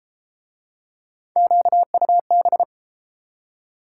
Morse code, heard as a keyed tone, about 29 wpm, 700 Hz. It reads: QUB